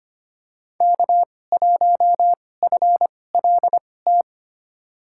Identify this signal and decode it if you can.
Morse code, heard as a keyed tone, about 25 wpm, spaced as for 18 wpm, 700 Hz. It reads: K1FLT
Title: USA Call Signs - 5 Characters - 2x Character Spacing - 25wpm